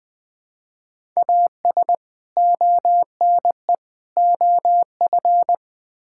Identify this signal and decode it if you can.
Morse code, heard as a keyed tone, about 20 wpm, 700 Hz. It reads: AS ONE OF
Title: Build A Phrase - 5-Word Phrases - 20wpm